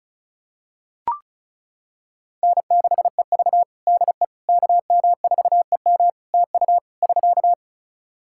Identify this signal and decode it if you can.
Morse code, heard as a keyed tone, about 36 wpm, 700 Hz. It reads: N6EV DE KM4EM TU <SK>
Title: Practice QSO - Exchange 4 - Ending - 35wpm